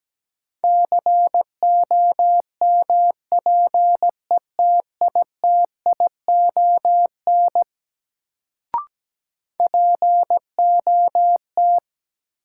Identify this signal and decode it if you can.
Morse code, heard as a keyed tone, about 17 wpm, 700 Hz. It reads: COMPETITION POT
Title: Sentences from Top 1000 Words - New Word Drill - 17wpm